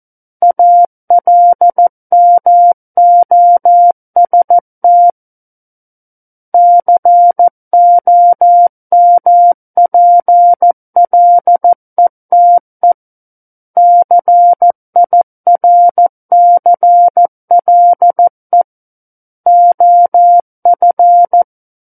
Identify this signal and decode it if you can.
Morse code, heard as a keyed tone, about 14 wpm, 700 Hz. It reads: ALMOST COMPLETE CIRCLE OF
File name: Book One / 05-Chapter 5 - 14 WPM